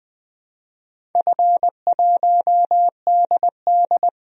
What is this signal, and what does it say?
Morse code, heard as a keyed tone, about 20 wpm, 700 Hz. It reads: F1DD
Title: DXCC-100 Call Signs - 4 Characters - Repeated 2x - 20wpm